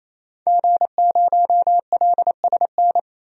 Morse code, heard as a keyed tone, about 28 wpm, 700 Hz. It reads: G0LSN
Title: DXCC-100 Call Signs - 5 Characters - 28wpm